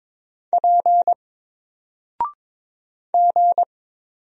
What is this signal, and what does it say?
Morse code, heard as a keyed tone, about 22 wpm, 700 Hz. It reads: P G
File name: Single-Letter-Number-22wpm